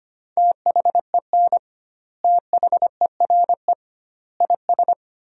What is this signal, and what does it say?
Morse code, heard as a keyed tone, about 25 wpm, 700 Hz. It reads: THEN THERE IS
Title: 2x Word-Spacing - Sentences from Top 100 Words - 25wpm